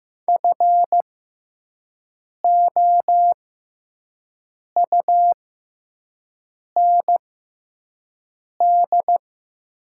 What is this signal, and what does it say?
Morse code, heard as a keyed tone, about 15 wpm, spaced as for 5 wpm, 700 Hz. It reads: FOUND
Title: 6x Character Spacing - Top 300 Words - 15wpm